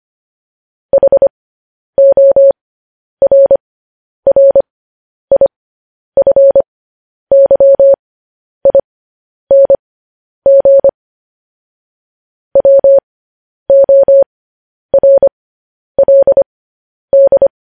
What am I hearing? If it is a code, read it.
Morse code, heard as a keyed tone, about 25 wpm, spaced as for 10 wpm, 550 Hz. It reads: HORRIFYING WORLD
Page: news-headlines-in-morse-code-at-10-wpm